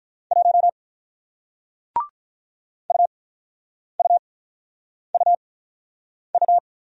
Morse code, heard as a keyed tone, about 42 wpm, spaced as for 9 wpm, 700 Hz. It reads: 1 UUUU
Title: Speed-Racing Single Letter-Number 35wpm